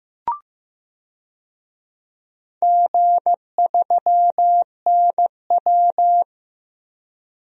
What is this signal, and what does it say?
Morse code, heard as a keyed tone, about 15 wpm, 700 Hz. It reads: G3NW
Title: DXCC-100 Call Signs - 4 Characters - Repeated 3x - 15wpm